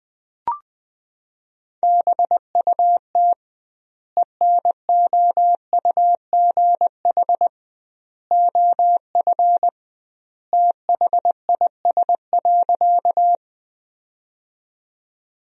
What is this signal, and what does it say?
Morse code, heard as a keyed tone, about 20 wpm, 700 Hz. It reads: BUT ENOUGH OF THIS.
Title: 2x Word-Spacing - Sentences from Top 400 Words - 20wpm